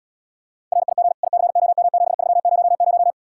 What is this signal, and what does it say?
Morse code, heard as a keyed tone, about 79 wpm, 700 Hz. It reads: RIG ICOM7300